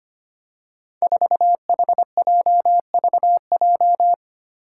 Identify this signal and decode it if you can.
Morse code, heard as a keyed tone, about 25 wpm, 700 Hz. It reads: 4HJVJ
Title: Five Challenging Letter-Number Combinations 25wpm